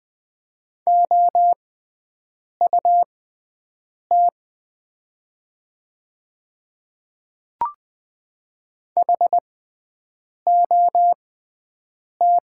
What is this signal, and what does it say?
Morse code, heard as a keyed tone, about 20 wpm, spaced as for 7 wpm, 700 Hz. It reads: OUT HOT
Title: 6x Character Spacing - Top 100 Words - 20wpm